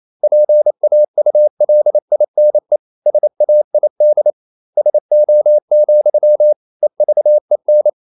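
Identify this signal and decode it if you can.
Morse code, heard as a keyed tone, about 28 wpm, 600 Hz. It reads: PAULINE SAID SO, EVEN
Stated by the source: Morse code practice at 28 wpm, text of Ken Follett's novel Never in English, from CW med Gnister.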